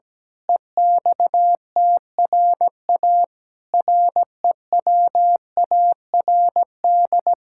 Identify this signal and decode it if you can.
Morse code, heard as a keyed tone, about 17 wpm, 700 Hz. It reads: EXTRA REWARD
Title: Pangram Challenge v2 - 17wpm